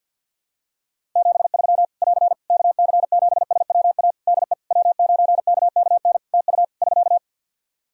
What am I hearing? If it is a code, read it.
Morse code, heard as a keyed tone, about 51 wpm, 700 Hz. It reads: 73 <AR> KC7SWA DE W9KKN TU <SK>
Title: Practice QSO - Exchange 4 - Ending - 50wpm